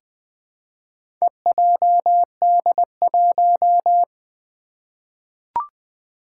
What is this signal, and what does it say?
Morse code, heard as a keyed tone, about 20 wpm, 700 Hz. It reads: EJD1
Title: Four Challenging Letter-Number Combinations 20wpm